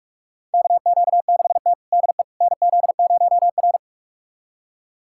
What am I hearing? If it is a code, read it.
Morse code, heard as a keyed tone, about 42 wpm, 700 Hz. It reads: KQ6T DE NZ0R
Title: USA Callsign DE Callsign - 4 Characters - 45wpm